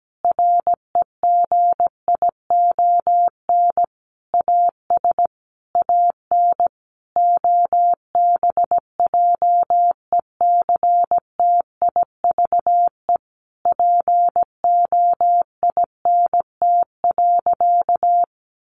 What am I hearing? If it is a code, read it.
Morse code, heard as a keyed tone, about 17 wpm, 700 Hz. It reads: REGION AS AN OBJECTIVE POINT.
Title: Sentences from Top 1500 Words 17wpm